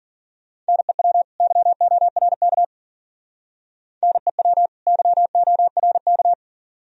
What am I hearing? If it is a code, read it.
Morse code, heard as a keyed tone, about 50 wpm, 700 Hz. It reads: NEW YORK NEW YORK
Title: Speed-Racing - Sets of 2 Words 40wpm